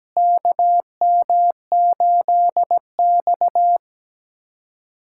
Morse code, heard as a keyed tone, about 17 wpm, 700 Hz. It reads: KM8X
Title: USA Callsign DE Callsign - 4 Characters - 17wpm